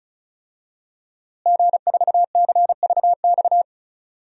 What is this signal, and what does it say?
Morse code, heard as a keyed tone, about 36 wpm, 700 Hz. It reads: G4CVX
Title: DXCC-100 Call Signs - 5 Characters - Repeated 2x - 35wpm